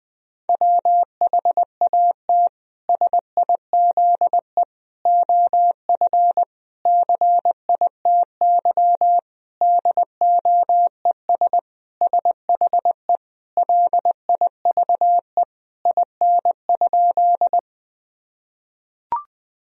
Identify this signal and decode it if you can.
Morse code, heard as a keyed tone, about 21 wpm, 700 Hz. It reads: WHAT SIZE OF CITY DOES SHE LIVE IN?
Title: Sentences from Top 1000 Words - Review Part 2 - 20wpm